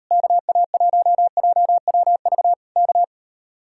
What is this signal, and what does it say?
Morse code, heard as a keyed tone, about 34 wpm, 700 Hz. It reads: KA1JWV K